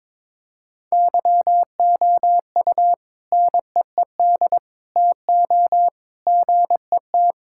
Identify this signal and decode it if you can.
Morse code, heard as a keyed tone, about 22 wpm, 700 Hz. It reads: YOU NEED TO GET
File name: practice / 22wpm / Sentences with 3-Word Phrases - 22wpm